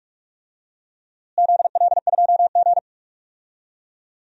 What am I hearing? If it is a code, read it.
Morse code, heard as a keyed tone, about 44 wpm, 700 Hz. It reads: ZL2G
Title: DXCC-100 Call Signs - 4 Characters - Repeated 2x - 45wpm